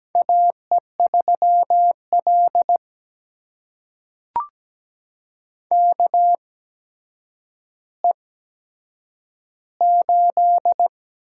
Morse code, heard as a keyed tone, about 17 wpm, spaced as for 8 wpm, 700 Hz. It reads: AE3L KE8
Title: USA Call Signs - 8x Character Spacing - 17wpm